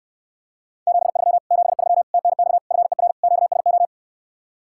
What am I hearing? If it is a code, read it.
Morse code, heard as a keyed tone, about 69 wpm, 700 Hz. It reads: 73 73 TNX FER QSO